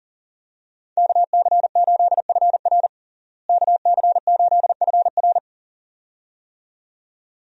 Morse code, heard as a keyed tone, about 40 wpm, 700 Hz. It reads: KC8FR KC8FR
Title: USA Call Signs - 5 Characters - Repeated 2x - 40wpm